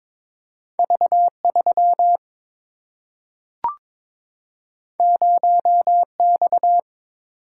Morse code, heard as a keyed tone, about 22 wpm, 700 Hz. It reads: V3 0X
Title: Single-Letter Single-Number 22wpm